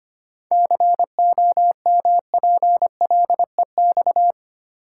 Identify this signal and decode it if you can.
Morse code, heard as a keyed tone, about 25 wpm, 700 Hz. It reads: COMPLEX
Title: Seven Letter Words - 25wpm